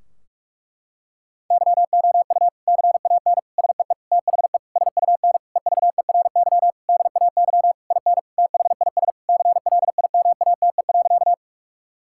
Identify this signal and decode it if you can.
Morse code, heard as a keyed tone, about 48 wpm, 700 Hz. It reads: YOU CAN SEE THE SUN EVERY DAY IN THIS CLIMATE.
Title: Sentences from Top 1600 Words 45wpm